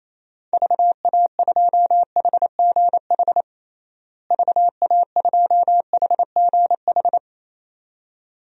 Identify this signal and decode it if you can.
Morse code, heard as a keyed tone, about 28 wpm, 700 Hz. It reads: VA2HGH VA2HGH
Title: DXCC-100 Call Signs - 6 Characters - Repeated 2x - 28wpm